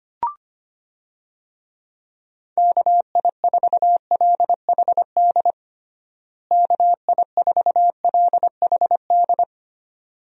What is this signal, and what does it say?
Morse code, heard as a keyed tone, about 25 wpm, 700 Hz. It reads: KI4LHD KI4LHD
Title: USA Call Signs - 6 Characters - Repeated 2x - 25wpm